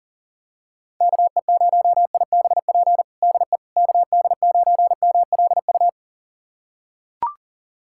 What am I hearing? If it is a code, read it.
Morse code, heard as a keyed tone, about 40 wpm, 700 Hz. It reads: KE0IBP DE KD9MLU
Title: USA Callsign DE Callsign - 6 Characters - 40wpm